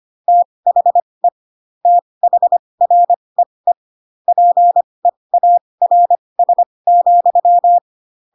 Morse code, heard as a keyed tone, about 25 wpm, spaced as for 20 wpm, 700 Hz. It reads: THE THREE PEARS,
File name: Pinocchio / Pinocchio ch7 15 25 ik1hge